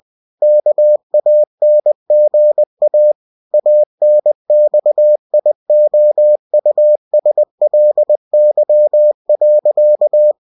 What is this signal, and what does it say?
Morse code, heard as a keyed tone, about 20 wpm, 600 Hz. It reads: KANGA ANXIOUSLY.